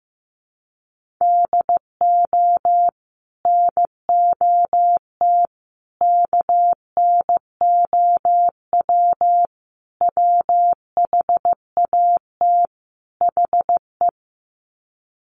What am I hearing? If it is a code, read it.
Morse code, heard as a keyed tone, about 15 wpm, 700 Hz. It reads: DO NOT KNOW WHAT HE
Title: Sets of 5 Words Encore2 15wpm